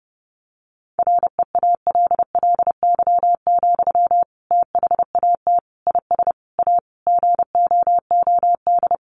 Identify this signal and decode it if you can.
Morse code, heard as a keyed tone, about 30 wpm, 700 Hz. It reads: REALLY, THAT IS A GOOD
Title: Sentences from Top 600 Words 30wpm